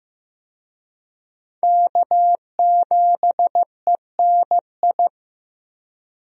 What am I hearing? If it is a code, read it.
Morse code, heard as a keyed tone, about 15 wpm, 700 Hz. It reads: K7ENI
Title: USA Call Signs - 5 Characters - Repeated 2x - 15wpm